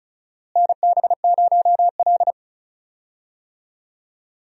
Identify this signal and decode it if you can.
Morse code, heard as a keyed tone, about 36 wpm, 700 Hz. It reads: NB0L
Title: USA Call Signs - 4 Characters - Repeated 2x - 35wpm